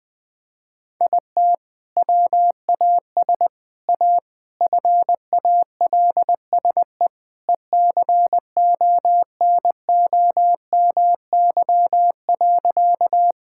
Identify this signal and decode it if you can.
Morse code, heard as a keyed tone, about 21 wpm, 700 Hz. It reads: IT WAS A FALSE ECONOMY.
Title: Sentences from Top 1600 Words 20wpm